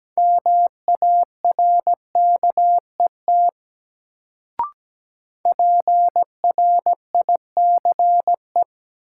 Morse code, heard as a keyed tone, about 17 wpm, 700 Hz. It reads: MARKET PRICE
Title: Sentences from Top 600 Words - New Word Drill - 17wpm